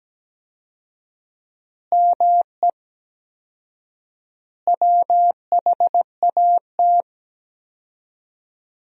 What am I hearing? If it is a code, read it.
Morse code, heard as a keyed tone, about 17 wpm, 700 Hz. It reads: ME WHAT
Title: 4x Word-Spacing - Sentences from Top 500 Words - 17wpm